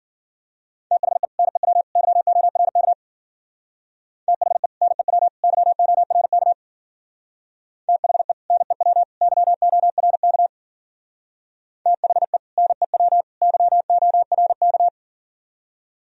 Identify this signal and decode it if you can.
Morse code, heard as a keyed tone, about 50 wpm, 700 Hz. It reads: THE NEW YORK THE NEW YORK THE NEW YORK THE NEW YORK
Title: Speed-Racing - Sets of 3 Words 40wpm